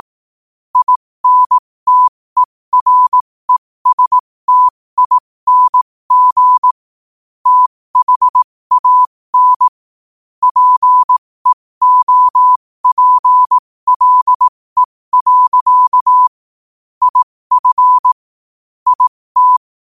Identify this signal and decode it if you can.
Morse code, heard as a keyed tone, about 18 wpm, spaced as for 15 wpm, 1000 Hz. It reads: INTERESTING THAN PEOPLE. IF IT
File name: qotdfiles / 15wpm